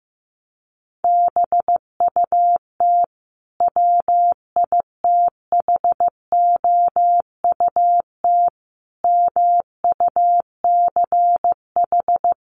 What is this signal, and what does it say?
Morse code, heard as a keyed tone, about 15 wpm, 700 Hz. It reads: BUT WITHOUT MUCH
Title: Sentences from Top 900 Words 15wpm